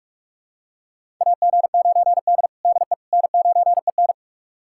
Morse code, heard as a keyed tone, about 42 wpm, 700 Hz. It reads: AG9D DE N9EN
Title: USA Callsign DE Callsign - 4 Characters - 45wpm